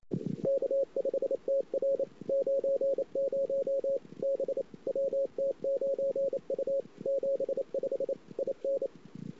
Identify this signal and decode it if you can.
Morse code, heard as a keyed tone, about 28 wpm, 550 Hz. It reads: K5TR 90 B WT9U 75 IN
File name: CW / 2008-ss-cw-k5tr-op-n5rz -22:00